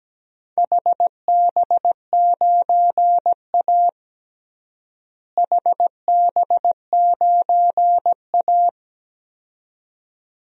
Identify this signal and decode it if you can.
Morse code, heard as a keyed tone, about 17 wpm, 700 Hz. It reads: HB9A HB9A
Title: DXCC-100 Call Signs - 4 Characters - Repeated 2x - 17wpm